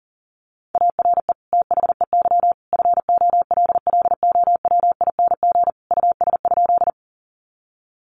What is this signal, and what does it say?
Morse code, heard as a keyed tone, about 37 wpm, 700 Hz. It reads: ARE THEY FOLLOWING US?
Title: Sentences from Top 700 Words 40wpm